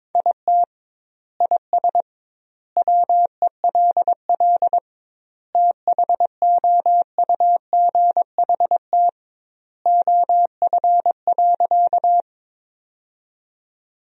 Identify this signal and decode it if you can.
Morse code, heard as a keyed tone, about 22 wpm, 700 Hz. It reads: IT IS WELL THOUGHT OF.
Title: 2x Word-Spacing - Sentences from Top 300 Words - 22wpm